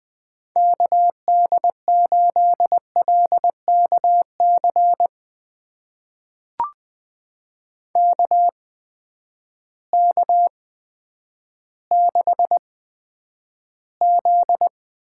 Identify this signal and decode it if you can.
Morse code, heard as a keyed tone, about 20 wpm, spaced as for 10 wpm, 700 Hz. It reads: KD8LKC KK6Z
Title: USA Call Signs - 6 Characters - 8x Character Spacing - 20wpm